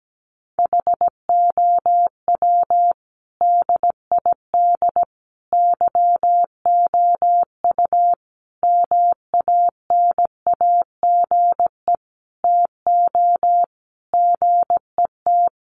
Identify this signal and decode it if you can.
Morse code, heard as a keyed tone, about 17 wpm, 700 Hz. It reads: HOW DID YOU MANAGE TO GET
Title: Sentences from Top 1400 Words 17wpm